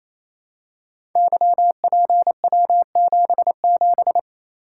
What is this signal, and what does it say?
Morse code, heard as a keyed tone, about 27 wpm, 690 Hz. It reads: YPW77
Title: Code Groups - 5 Letter:Numbers - 28wpm